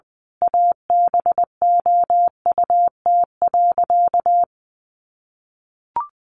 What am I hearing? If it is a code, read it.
Morse code, heard as a keyed tone, about 20 wpm, 700 Hz. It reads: ABOUT.
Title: Sentences from Top 1300 Words 20wpm